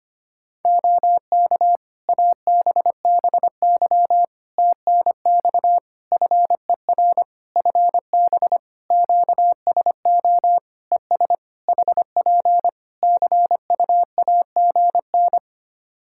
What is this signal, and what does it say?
Morse code, heard as a keyed tone, about 25 wpm, 700 Hz. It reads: OK ABBY TNX FER FB QSO ES HP CUAGN
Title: Practice QSO - Exchange 4 - Ending - 25wpm